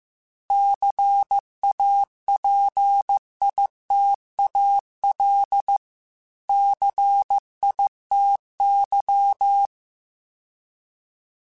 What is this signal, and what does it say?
Morse code, heard as a keyed tone, about 15 wpm, 800 Hz. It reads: CAPITAL CITY